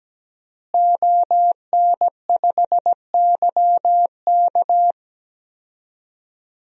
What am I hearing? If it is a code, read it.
Morse code, heard as a keyed tone, about 17 wpm, 700 Hz. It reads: ON5YK
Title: DXCC-100 Call Signs - 5 Characters - Repeated 2x - 17wpm